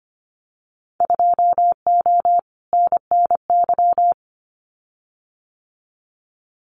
Morse code, heard as a keyed tone, about 25 wpm, 700 Hz. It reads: 2O NNY
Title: Winter Field Day Exchanges 25wpm